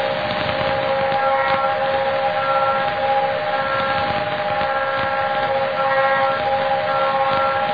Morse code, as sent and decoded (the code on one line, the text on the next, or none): none